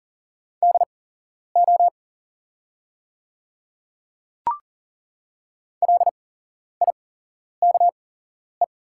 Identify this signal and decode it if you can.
Morse code, heard as a keyed tone, about 41 wpm, spaced as for 11 wpm, 690 Hz. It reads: DO LIKE
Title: 8x Character Spacing - Top 100 Words - 40wpm